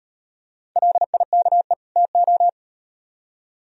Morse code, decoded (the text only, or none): LIKE TO